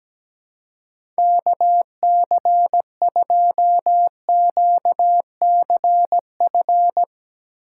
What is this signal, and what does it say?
Morse code, heard as a keyed tone, about 17 wpm, 700 Hz. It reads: KC2QCF
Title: USA Call Signs - 2x Character Spacing - 17wpm